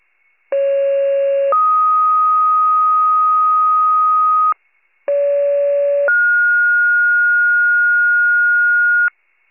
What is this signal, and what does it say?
Two-tone page, heard as a one-second tone, then a three-second tone, about 550 then 1250 Hz, about 570 then 1430 Hz, with noise. It is heard from 0.5 to 9.1 s.